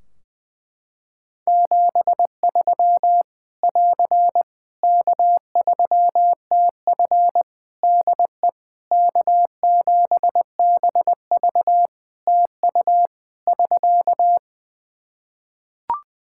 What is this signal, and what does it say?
Morse code, heard as a keyed tone, about 21 wpm, 700 Hz. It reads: 73 <AR> K3TF DE K7BV TU <SK>
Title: Practice QSO - Exchange 4 - Ending - 20wpm